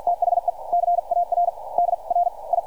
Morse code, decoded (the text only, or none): THE CAR DRI